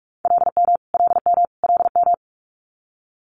Morse code, heard as a keyed tone, about 46 wpm, 700 Hz. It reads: LM LM LM
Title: Multi-letter Phonemes Repeated 3x 45wpm